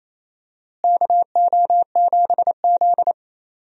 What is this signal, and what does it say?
Morse code, heard as a keyed tone, about 28 wpm, 700 Hz. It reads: KO7Z
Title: USA Call Signs - 4 Characters - Repeated 2x - 28wpm